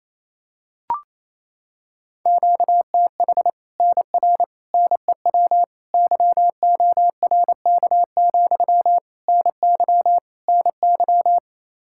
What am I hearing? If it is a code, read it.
Morse code, heard as a keyed tone, about 27 wpm, 700 Hz. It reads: QTH NR NEW YORK, NY NY
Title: Practice QSO - Exchange 1 - Beginning - 28wpm